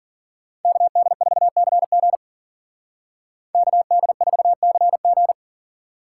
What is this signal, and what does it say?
Morse code, heard as a keyed tone, about 52 wpm, 700 Hz. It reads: KD4CG KD4CG